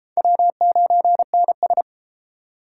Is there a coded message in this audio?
Yes: Morse code from 0.2 to 1.8 s, heard as a keyed tone, about 30 wpm, 700 Hz. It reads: W9NS